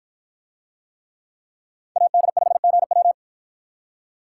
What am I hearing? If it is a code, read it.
Morse code, heard as a keyed tone, about 53 wpm, 700 Hz. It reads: AD5GW